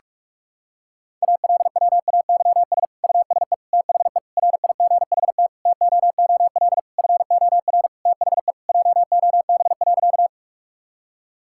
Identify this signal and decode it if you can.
Morse code, heard as a keyed tone, about 46 wpm, 700 Hz. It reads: ALWAYS USE THE RIGHT TOOL FOR THE JOB.
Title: Sentences from Top 1000 Words - Review Part 6 - 45wpm